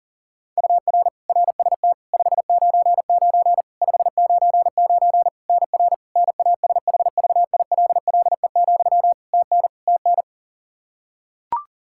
Morse code, heard as a keyed tone, about 40 wpm, 700 Hz. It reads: UR RST 599 599 NR NASHVILLE, TN TN
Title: Practice QSO - Exchange 1 - Beginning - 40wpm